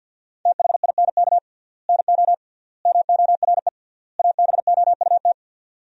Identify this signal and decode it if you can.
Morse code, heard as a keyed tone, about 50 wpm, 700 Hz. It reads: THINK NO MORE ABOUT
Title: 3x Word-Spacing - Sentences from Top 200 Words - 50wpm